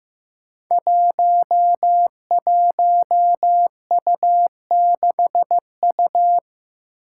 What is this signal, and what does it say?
Morse code, heard as a keyed tone, about 15 wpm, 700 Hz. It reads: 11U6U
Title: Five Challenging Letter-Number Combinations 15wpm